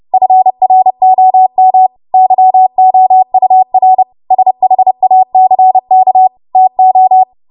Morse code, heard as a keyed tone, about 30 wpm, 750 Hz. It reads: FROM YOUR SHACK TO
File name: Morse / 30WPM